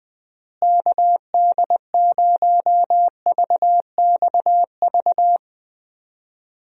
Morse code, heard as a keyed tone, about 20 wpm, 700 Hz. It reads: KD0VXV
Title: USA Call Signs - 6 Characters - Repeated 3x - 20wpm